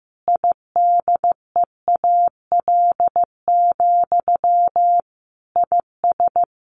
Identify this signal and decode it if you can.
Morse code, heard as a keyed tone, about 15 wpm, 700 Hz. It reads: IDEAL, IS
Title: Sentences from Top 1400 Words 15wpm